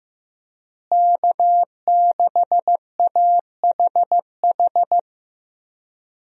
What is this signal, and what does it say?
Morse code, heard as a keyed tone, about 15 wpm, 700 Hz. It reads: K6AHH
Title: USA Call Signs - 5 Characters - Repeated 3x - 15wpm